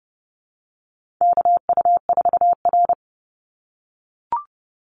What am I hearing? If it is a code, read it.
Morse code, heard as a keyed tone, about 30 wpm, 700 Hz. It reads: KU4R
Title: NA SOTA Chaser Callsigns 30wpm